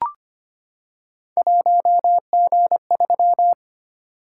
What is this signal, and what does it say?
Morse code, heard as a keyed tone, about 25 wpm, 700 Hz. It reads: 1G3
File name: Three Letter-Number Combinations - 25wpm